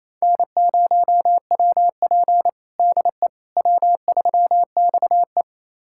Morse code, heard as a keyed tone, about 28 wpm, 690 Hz. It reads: N0WP DE W3XE
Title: USA Callsign DE Callsign - 4 Characters - 28wpm